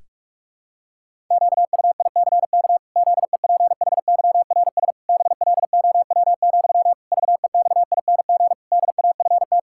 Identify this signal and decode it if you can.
Morse code, heard as a keyed tone, about 41 wpm, 700 Hz. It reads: QUICK ZEPHYRS BLOW, VEXING DAFT
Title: Pangram Challenge v2 - 45wpm